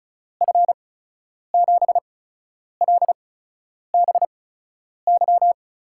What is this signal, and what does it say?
Morse code, heard as a keyed tone, about 36 wpm, spaced as for 10 wpm, 700 Hz. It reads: F7LBY
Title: USA Call Signs - 6 Characters - 8x Character Spacing - 35wpm